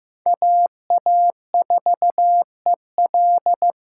Morse code, heard as a keyed tone, about 15 wpm, 700 Hz. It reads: AA4EL
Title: USA Call Signs - 5 Characters - 3x Character Spacing - 15wpm